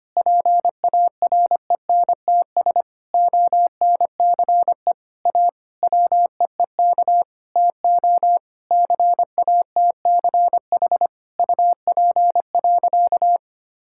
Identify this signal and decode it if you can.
Morse code, heard as a keyed tone, about 25 wpm, 700 Hz. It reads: PARENTS ONCE A WEEK TO CATCH UP.
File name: practice / 25wpm / Difficult Sentences with Hints - 25wpm